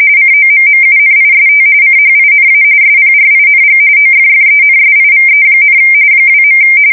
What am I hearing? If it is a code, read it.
RTTY, about 45.45 baud. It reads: COMMUNIST CHINA SPRING OFFENSIVE, 2.
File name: fsk170 / USS-New-Jersey